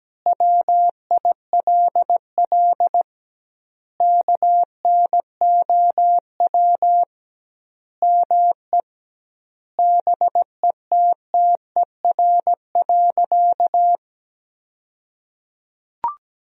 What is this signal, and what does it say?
Morse code, heard as a keyed tone, about 17 wpm, 700 Hz. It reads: WILL KNOW ME BETTER.
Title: 2x Word-Spacing - Sentences from Top 400 Words - 17wpm